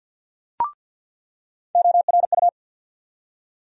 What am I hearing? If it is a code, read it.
Morse code, heard as a keyed tone, about 49 wpm, 700 Hz. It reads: ORU
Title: USA Call Sign Postfixes - 3 Letters 50wpm